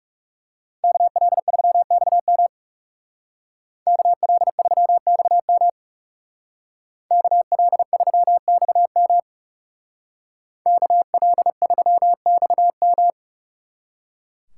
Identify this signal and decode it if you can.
Morse code, heard as a keyed tone, about 39 wpm, 700 Hz. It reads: KL3XM KL3XM KL3XM KL3XM